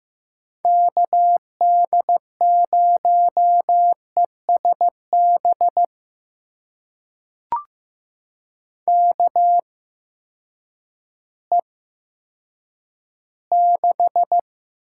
Morse code, heard as a keyed tone, about 15 wpm, spaced as for 9 wpm, 700 Hz. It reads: KD0ESB KE6
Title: USA Call Signs - 6 Characters - 8x Character Spacing - 15wpm